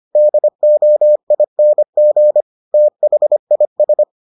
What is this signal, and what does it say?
Morse code, heard as a keyed tone, about 25 wpm, 600 Hz. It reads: DOING THIS